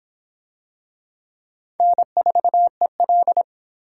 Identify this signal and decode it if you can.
Morse code, heard as a keyed tone, about 24 wpm, 700 Hz. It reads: N4EL